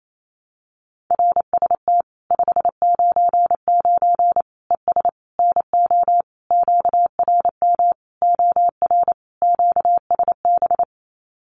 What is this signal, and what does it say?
Morse code, heard as a keyed tone, about 28 wpm, 700 Hz. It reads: RST 599 ES NO QRM OR QSB
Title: Common QSO Phrases 28wpm